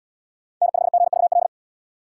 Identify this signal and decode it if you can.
Morse code, heard as a keyed tone, about 68 wpm, 690 Hz. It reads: N5GVL